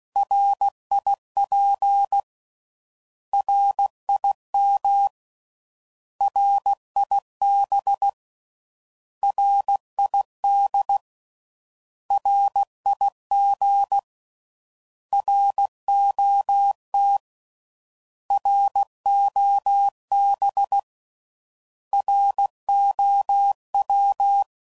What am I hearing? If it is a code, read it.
Morse code, heard as a keyed tone, about 16 wpm, 800 Hz. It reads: RIP RIM RIB RID RIG ROT ROB ROW
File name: Prefixes / By First Letter / Rs